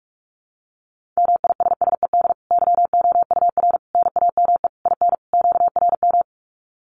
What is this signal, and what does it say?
Morse code, heard as a keyed tone, about 44 wpm, 700 Hz. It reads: MISSED YOUR NAME IN QRM
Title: Common QSO Phrases 45wpm